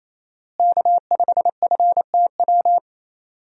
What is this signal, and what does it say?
Morse code, heard as a keyed tone, about 28 wpm, 700 Hz. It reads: K5FTW
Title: USA Call Signs - 5 Characters - Repeated 3x - 28wpm